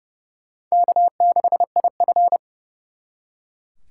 Morse code, heard as a keyed tone, about 37 wpm, 700 Hz. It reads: K6IF
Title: Speed Racing - USA Call Signs 30wpm